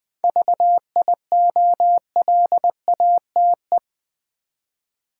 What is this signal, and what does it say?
Morse code, heard as a keyed tone, about 20 wpm, 700 Hz. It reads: VIOLATE
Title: 7 Letter Words - First 3 Letters Given - 20wpm